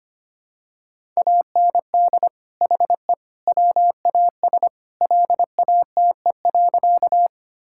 Morse code, heard as a keyed tone, about 25 wpm, 700 Hz. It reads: AND HE WAS LATE.
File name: practice / 25wpm / Sentences with 3-Word Phrases - 25wpm